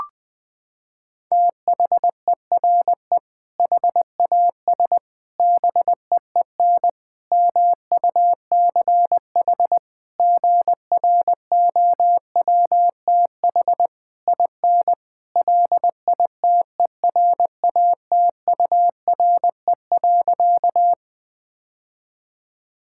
Morse code, heard as a keyed tone, about 20 wpm, 700 Hz. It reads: THERE HAS BEEN MUCH GROWTH IN LITERATURE.
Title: Sentences from Top 1000 Words - Review Part 5 - 20wpm